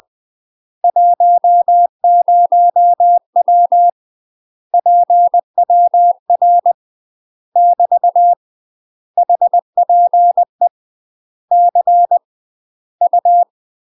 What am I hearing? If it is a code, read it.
Morse code, heard as a keyed tone, about 20 wpm, 700 Hz. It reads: 10W PWR <BT> HPE C U
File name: MASTER QSOs TRAINING/20WPM